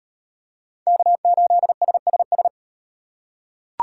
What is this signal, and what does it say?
Morse code, heard as a keyed tone, about 38 wpm, 700 Hz. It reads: K8SSS